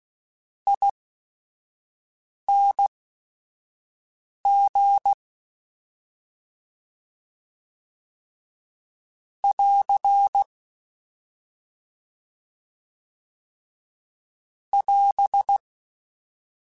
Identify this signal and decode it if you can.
Morse code, heard as a keyed tone, about 16 wpm, spaced as for 5 wpm, 800 Hz. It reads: ING <AR> <AS>
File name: Short Runs / Short Runs-mp3s / ESE ING @5